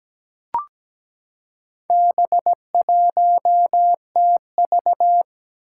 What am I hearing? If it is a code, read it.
Morse code, heard as a keyed tone, about 17 wpm, 700 Hz. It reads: B1TV
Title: Four Challenging Letter-Number Combinations 17wpm